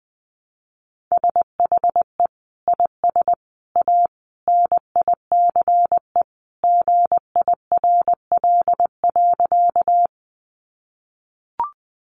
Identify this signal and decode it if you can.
Morse code, heard as a keyed tone, about 20 wpm, 700 Hz. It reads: SHE IS A NICE GIRL.
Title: Sentences from Top 1100 Words 20wpm